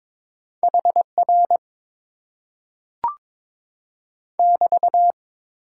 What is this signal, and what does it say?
Morse code, heard as a keyed tone, about 22 wpm, 700 Hz. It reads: HR <BT>
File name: Top 90 QSO Elements - 6x Character Spacing - 22wpm